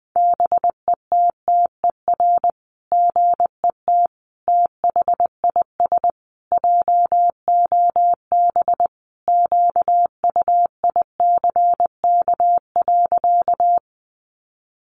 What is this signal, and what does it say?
Morse code, heard as a keyed tone, about 19 wpm, 700 Hz. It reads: BETTER GET THIS JOB QUICK.
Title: Sentences from Top 600 Words 20wpm